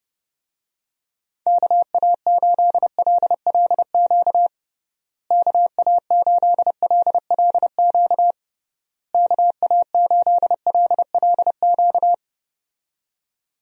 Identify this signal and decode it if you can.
Morse code, heard as a keyed tone, about 32 wpm, 700 Hz. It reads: KA8LLQ KA8LLQ KA8LLQ
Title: USA Call Signs - 6 Characters - Repeated 3x - 30wpm